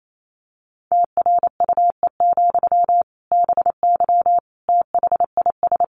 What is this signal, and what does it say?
Morse code, heard as a keyed tone, about 28 wpm, 700 Hz. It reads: TRUE, BY THIS
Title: Sentences from Top 1500 Words 28wpm